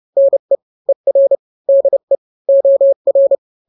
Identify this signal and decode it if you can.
Morse code, heard as a keyed tone, about 30 wpm, 550 Hz. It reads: NE ER DE OR
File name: n-grams / 2-grams 30 25 WPM